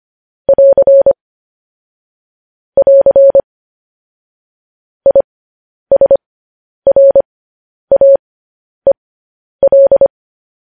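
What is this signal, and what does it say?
Morse code, heard as a keyed tone, about 25 wpm, spaced as for 10 wpm, 550 Hz. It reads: <AR> <AR> ISRAEL
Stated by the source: Morse code news headlines broadcast at 10 wpm.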